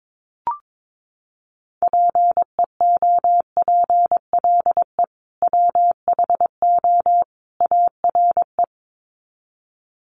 Morse code, heard as a keyed tone, about 22 wpm, 700 Hz. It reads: PEOPLE WHO ARE
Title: Sets of 3 Words 22wpm